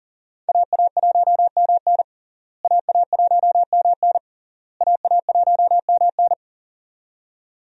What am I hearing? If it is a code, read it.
Morse code, heard as a keyed tone, about 40 wpm, 700 Hz. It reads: AA1MN AA1MN AA1MN
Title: USA Call Signs - 5 Characters - Repeated 3x - 40wpm